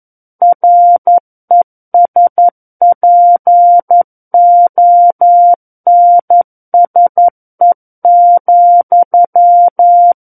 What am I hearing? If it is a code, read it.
Morse code, heard as a keyed tone, about 11 wpm, 700 Hz. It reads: RESPONSE,
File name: 11 WPM